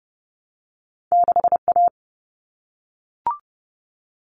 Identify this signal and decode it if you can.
Morse code, heard as a keyed tone, about 30 wpm, 700 Hz. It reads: 6A
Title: All Cut Numbers - 2-Digit 30wpm